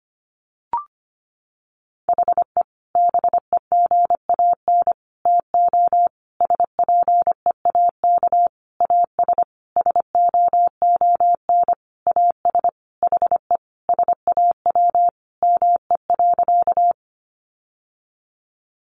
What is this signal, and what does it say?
Morse code, heard as a keyed tone, about 25 wpm, 700 Hz. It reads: HE BEGAN TO SPEAK AS SOON AS HE SAW ME.
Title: Sentences from Top 1100 Words 25wpm